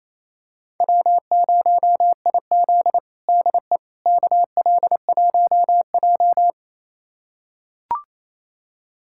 Morse code, heard as a keyed tone, about 28 wpm, 700 Hz. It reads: W0IZ DE KL1J
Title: USA Callsign DE Callsign - 4 Characters - 28wpm